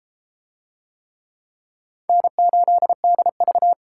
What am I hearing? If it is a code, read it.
Morse code, heard as a keyed tone, about 31 wpm, 700 Hz. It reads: N8DV